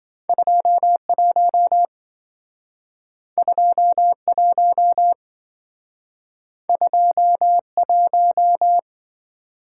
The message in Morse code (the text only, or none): 21 21 21